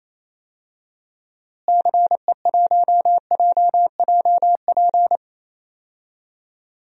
Morse code, heard as a keyed tone, about 28 wpm, 700 Hz. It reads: CE1JJP
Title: DXCC-100 Call Signs - 6 Characters - Repeated 2x - 28wpm